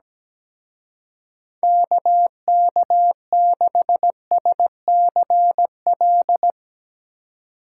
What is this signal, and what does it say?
Morse code, heard as a keyed tone, about 17 wpm, 700 Hz. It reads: KK6SCL
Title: Speed Racing - USA 2x3 Call Signs 17wpm